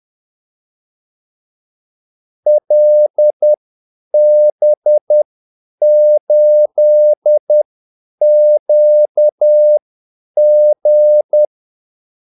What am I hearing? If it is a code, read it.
Morse code, heard as a keyed tone, about 10 wpm, spaced as for 8 wpm, 600 Hz. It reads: LB8QG
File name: Callsigns-10-8wpm